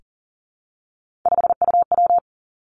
Morse code, heard as a keyed tone, about 38 wpm, 700 Hz. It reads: 5UW
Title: Three Challenging Letter-Number Combinations 40wpm